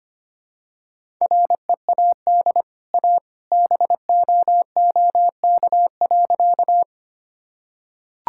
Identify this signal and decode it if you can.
Morse code, heard as a keyed tone, about 25 wpm, 700 Hz. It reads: READ A BOOK.
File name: Easy Sentences with Hints - 25wpm